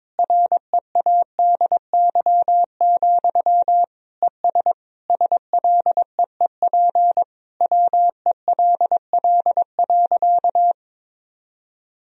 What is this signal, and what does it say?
Morse code, heard as a keyed tone, about 22 wpm, 700 Hz. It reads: READY, ES SLEEP WELL.
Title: Sentences from Top 500 Words and Abbreviations 22wpm